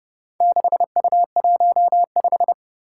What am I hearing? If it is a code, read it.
Morse code, heard as a keyed tone, about 30 wpm, 700 Hz. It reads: 6U15
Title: Four Challenging Letter-Number Combinations 30wpm